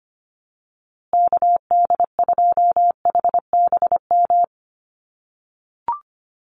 Morse code, heard as a keyed tone, about 25 wpm, 700 Hz. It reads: KD2HBM
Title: USA Call Signs Repeated 3x 25wpm